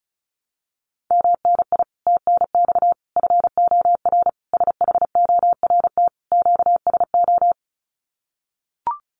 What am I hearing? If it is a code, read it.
Morse code, heard as a keyed tone, about 34 wpm, 700 Hz. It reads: MNI TNX FOR SHORT QSO
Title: Common QSO Phrases 35wpm